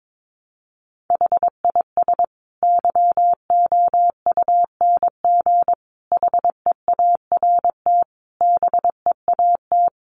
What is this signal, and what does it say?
Morse code, heard as a keyed tone, about 22 wpm, 700 Hz. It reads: HIS YOUNG HEART BEAT
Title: Sentences from Top 1300 Words 22wpm